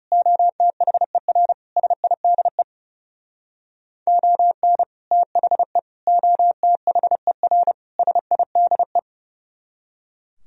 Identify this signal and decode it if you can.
Morse code, heard as a keyed tone, about 40 wpm, 700 Hz. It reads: OTHER SIDE ON THE OTHER SIDE